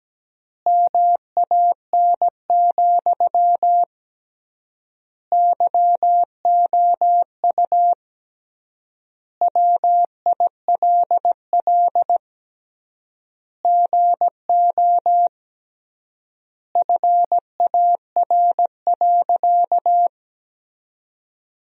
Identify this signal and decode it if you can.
Morse code, heard as a keyed tone, about 17 wpm, 700 Hz. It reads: MAN, YOU WILL GO FAR.
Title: 3x Word-Spacing - Sentences from Top 400 Words - 17wpm